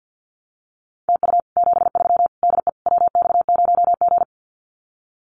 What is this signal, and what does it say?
Morse code, heard as a keyed tone, about 50 wpm, 700 Hz. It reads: TU 73 DE WX0G
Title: Common QSO Phrases Encore 50wpm